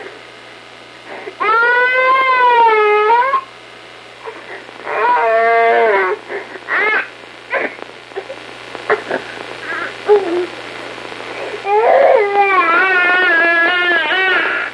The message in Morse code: none